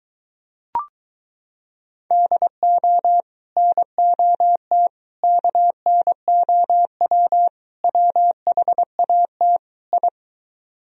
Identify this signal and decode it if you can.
Morse code, heard as a keyed tone, about 24 wpm, 700 Hz. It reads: DO NOT KNOW WHAT I